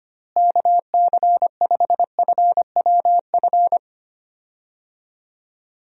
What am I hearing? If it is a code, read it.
Morse code, heard as a keyed tone, about 25 wpm, 700 Hz. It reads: KC5FWF
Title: USA Call Signs - 6 Characters - Repeated 2x - 25wpm